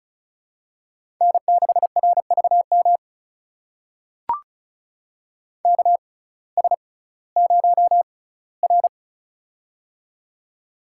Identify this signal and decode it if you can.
Morse code, heard as a keyed tone, about 36 wpm, spaced as for 19 wpm, 700 Hz. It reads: N6RVM KS0R
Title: USA Call Signs - 6x Character Spacing - 35wpm